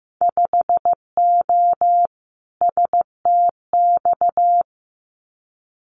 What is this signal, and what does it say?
Morse code, heard as a keyed tone, about 15 wpm, 700 Hz. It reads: 5O STX
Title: Winter Field Day Exchanges 15wpm